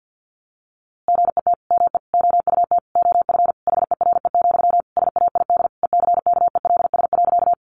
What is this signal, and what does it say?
Morse code, heard as a keyed tone, about 50 wpm, 700 Hz. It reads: GET ME OUT OF HERE, SAID ECKELS.
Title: A Sound Of Thunder 50wpm